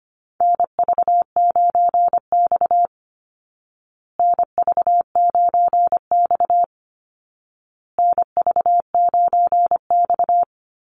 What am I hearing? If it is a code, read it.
Morse code, heard as a keyed tone, about 25 wpm, 700 Hz. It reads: NV9X NV9X NV9X
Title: Call Signs Repeated 2x - 25wpm